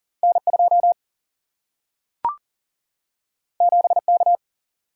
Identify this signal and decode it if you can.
Morse code, heard as a keyed tone, about 39 wpm, 700 Hz. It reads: N2 7K
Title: Single-Letter Single-Number 40wpm